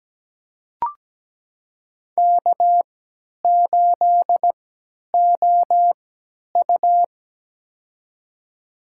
Morse code, heard as a keyed tone, about 17 wpm, spaced as for 10 wpm, 700 Hz. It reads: K8OU
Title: USA Call Signs - 4 Characters - 3x Character Spacing - 17wpm